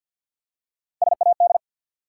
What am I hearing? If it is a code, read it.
Morse code, heard as a keyed tone, about 51 wpm, 700 Hz. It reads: SAD
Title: Three Letter Words - 50wpm